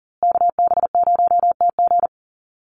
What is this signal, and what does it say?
Morse code, heard as a keyed tone, about 40 wpm, 700 Hz. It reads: KB0TG
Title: USA Call Signs Repeated 3x 40wpm